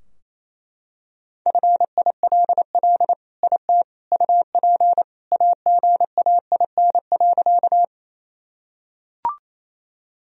Morse code, heard as a keyed tone, about 28 wpm, 700 Hz. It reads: FILL IT UP AGAIN.
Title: Sentences from Top 1000 Words - Review Part 3 - 28wpm